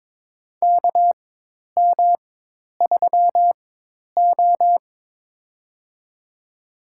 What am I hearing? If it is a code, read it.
Morse code, heard as a keyed tone, about 23 wpm, spaced as for 10 wpm, 700 Hz. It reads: KM3O
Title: USA Call Signs - 4 Characters - 4x Character Spacing - 22wpm